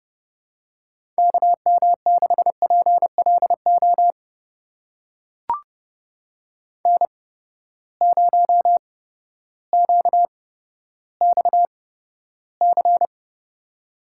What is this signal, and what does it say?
Morse code, heard as a keyed tone, about 29 wpm, spaced as for 14 wpm, 700 Hz. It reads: KM6PLO N0QXC
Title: USA Call Signs - 8x Character Spacing - 30wpm